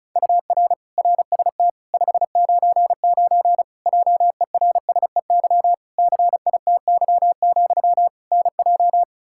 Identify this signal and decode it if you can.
Morse code, heard as a keyed tone, about 36 wpm, 700 Hz. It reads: UR RST 599 JERSEY CITY, NJ
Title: Practice QSO - Exchange 1 - Beginning - 35wpm